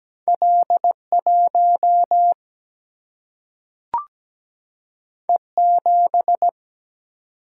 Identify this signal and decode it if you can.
Morse code, heard as a keyed tone, about 17 wpm, 700 Hz. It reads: L 1 E 7
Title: Single-Letter Single-Number 17wpm